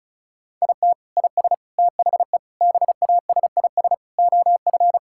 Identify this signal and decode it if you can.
Morse code, heard as a keyed tone, about 36 wpm, 700 Hz. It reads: IT IS THE BASIS OF
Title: Sentences from Top 1000 Words - Review Part 5 - 35wpm